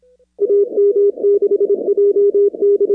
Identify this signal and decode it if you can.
Morse code, heard as a keyed tone, about 26 wpm, 400 Hz. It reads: AM6JD
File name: contests / 2019-cw-wpx-kt5j-k5pi- -2330 0-24